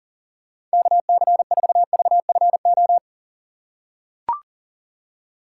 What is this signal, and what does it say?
Morse code, heard as a keyed tone, about 40 wpm, 700 Hz. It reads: KC4VFO
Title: USA Call Signs - 6 Characters - 4x Character Spacing - 40wpm